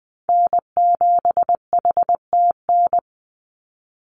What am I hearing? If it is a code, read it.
Morse code, heard as a keyed tone, about 20 wpm, 700 Hz. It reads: N7HTN
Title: USA Call Signs 20wpm